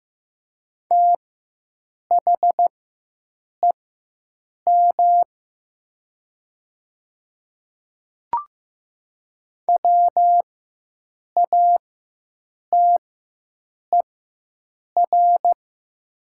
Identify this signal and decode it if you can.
Morse code, heard as a keyed tone, about 15 wpm, spaced as for 7 wpm, 700 Hz. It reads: THEM WATER
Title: 4x Character Spacing - Top 100 Words - 15wpm